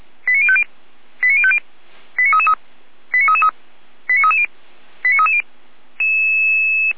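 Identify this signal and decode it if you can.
Five-tone selective call (ZVEI), heard as quick runs of five tones, tones of about 70 ms. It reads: 89050 89050 89333 89333 89330 89330
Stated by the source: ZVEI five-tone call received by SDR, from feuerwehr-orlamuende.org (Feuerwehr Orlamünde fire brigade) server.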